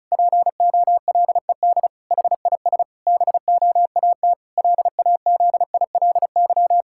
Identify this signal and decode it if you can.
Morse code, heard as a keyed tone, about 36 wpm, 700 Hz. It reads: POLED HIS BOAT LAZILY
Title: Pangram Challenge v2 - 35wpm